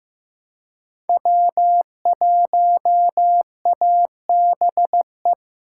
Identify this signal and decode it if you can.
Morse code, heard as a keyed tone, about 15 wpm, 700 Hz. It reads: W1ABE
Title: USA Call Signs - 4x Character Spacing - 15wpm